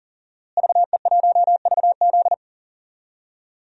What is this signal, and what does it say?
Morse code, heard as a keyed tone, about 37 wpm, 700 Hz. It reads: VE1VZ
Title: DXCC-100 Call Signs - 5 Characters - Repeated 3x - 40wpm